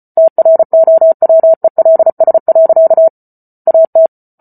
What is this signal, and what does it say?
Morse code, heard as a keyed tone, about 34 wpm, 650 Hz. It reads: TROWELS. AT